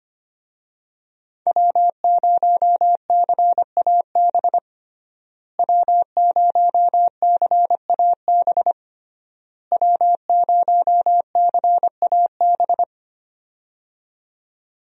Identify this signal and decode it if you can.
Morse code, heard as a keyed tone, about 25 wpm, 700 Hz. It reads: W0CAB W0CAB W0CAB
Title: USA Call Signs - 5 Characters - Repeated 3x - 25wpm